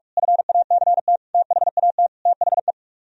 Morse code, heard as a keyed tone, about 49 wpm, 700 Hz. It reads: FACT THAT THE